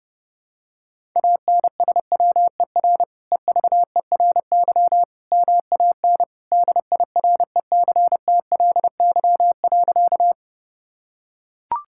Morse code, heard as a keyed tone, about 30 wpm, 700 Hz. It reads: ANSWER EVERY MAN DIRECTLY.
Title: Sentences from Top 1000 Words - Review Part 5 - 30wpm